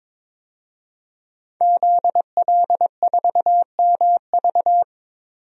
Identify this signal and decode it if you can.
Morse code, heard as a keyed tone, about 22 wpm, 700 Hz. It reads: ZL4MV
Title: DXCC-100 Call Signs - 5 Characters - 22wpm